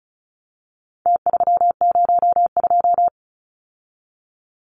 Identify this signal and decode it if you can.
Morse code, heard as a keyed tone, about 34 wpm, 700 Hz. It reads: T302
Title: Numbers - Contest Style - 4-Digit 35wpm